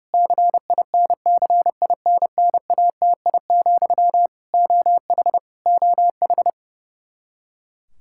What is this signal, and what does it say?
Morse code, heard as a keyed tone, about 30 wpm, 700 Hz. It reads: CINCINNATI, OH OH
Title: Practice QSO - Exchange 1 - Beginning - 30wpm